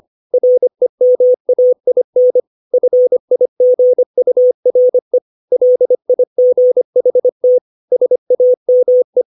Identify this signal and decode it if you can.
Morse code, heard as a keyed tone, about 25 wpm, 500 Hz. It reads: REMAIN FIGURE LIGHT SAME
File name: 632 words / 632 Words 25WPM